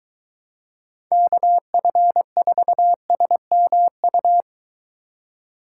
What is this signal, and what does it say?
Morse code, heard as a keyed tone, about 21 wpm, 700 Hz. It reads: KF4SMU